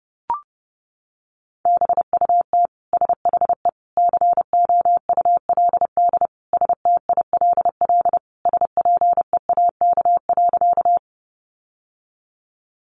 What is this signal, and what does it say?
Morse code, heard as a keyed tone, about 30 wpm, 700 Hz. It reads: BUT SHE COULD STILL SPEAK.
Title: Sentences from Top 1100 Words 30wpm